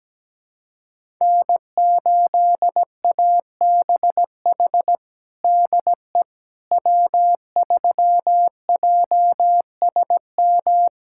Morse code, heard as a keyed tone, about 17 wpm, 700 Hz. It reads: N8ABH DE W3JSM
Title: USA Callsign DE Callsign - 5 Characters - 17wpm